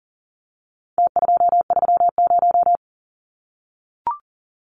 Morse code, heard as a keyed tone, about 40 wpm, 700 Hz. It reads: T230
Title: Numbers - Contest Style - 4-Digit 40wpm